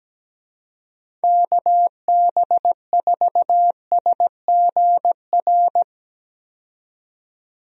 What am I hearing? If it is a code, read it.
Morse code, heard as a keyed tone, about 17 wpm, 700 Hz. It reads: KB4SGR
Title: USA Call Signs - 6 Characters - 17wpm